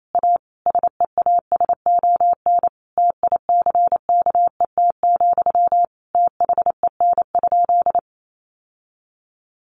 Morse code, heard as a keyed tone, about 28 wpm, 700 Hz. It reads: A SEASON TICKET, THEN?
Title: Sentences from Top 1400 Words 28wpm